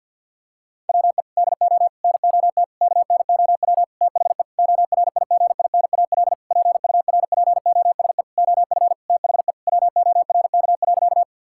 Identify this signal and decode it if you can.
Morse code, heard as a keyed tone, about 50 wpm, 700 Hz. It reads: WE DO NOT KNOW THE ORIGINAL PURPOSE OF THE WORK.
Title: Sentences from Top 1000 Words - Review Part 6 - 50wpm